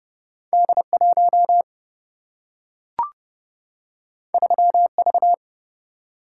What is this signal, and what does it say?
Morse code, heard as a keyed tone, about 29 wpm, 700 Hz. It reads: D1 3V
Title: Single-Letter Single-Number 30wpm